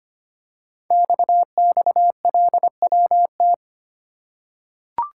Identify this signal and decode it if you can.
Morse code, heard as a keyed tone, about 25 wpm, 700 Hz. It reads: XXLWT
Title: Code Groups - 5 Letters - 25wpm